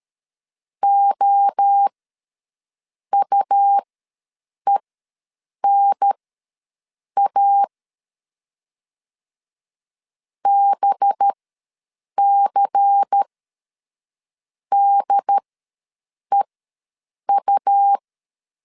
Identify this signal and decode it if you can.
Morse code, heard as a keyed tone, about 13 wpm, spaced as for 6 wpm, 800 Hz. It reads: OUENA BCDEU
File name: LEZIONI / 5 LBVC / 5 LBVC ws 12wpm 3159